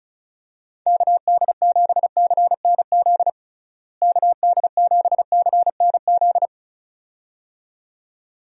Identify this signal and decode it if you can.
Morse code, heard as a keyed tone, about 36 wpm, 700 Hz. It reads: KD7CNZ KD7CNZ
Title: USA Call Signs - 6 Characters - Repeated 2x - 35wpm